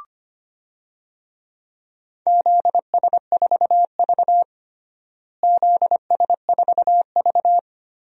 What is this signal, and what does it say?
Morse code, heard as a keyed tone, about 25 wpm, 700 Hz. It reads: ZS4V ZS4V
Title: DXCC-100 Call Signs - 4 Characters - Repeated 3x - 25wpm